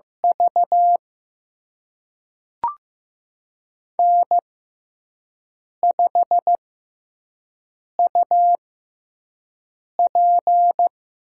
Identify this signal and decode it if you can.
Morse code, heard as a keyed tone, about 15 wpm, spaced as for 5 wpm, 700 Hz. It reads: V N5UP
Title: USA Call Signs - 6x Character Spacing - 15wpm